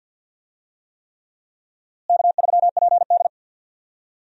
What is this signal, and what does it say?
Morse code, heard as a keyed tone, about 52 wpm, 700 Hz. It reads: K3PD
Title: USA Call Signs - 4 Characters - Repeated 3x - 50wpm